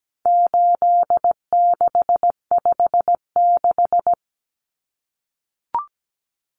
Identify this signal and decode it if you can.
Morse code, heard as a keyed tone, about 17 wpm, 700 Hz. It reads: 8656
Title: Common Cut Numbers - 4-Digit 17wpm